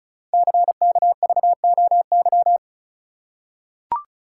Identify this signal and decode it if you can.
Morse code, heard as a keyed tone, about 36 wpm, 690 Hz. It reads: CKVOY
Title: Code Groups - 5 Letters - 35wpm